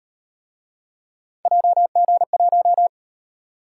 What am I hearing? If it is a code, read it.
Morse code, heard as a keyed tone, about 38 wpm, 700 Hz. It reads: JG1